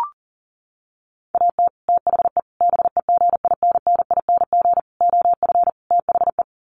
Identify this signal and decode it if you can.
Morse code, heard as a keyed tone, about 40 wpm, 700 Hz. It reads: AT THE BEGINNING OF THE
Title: Sets of 5 Words 40wpm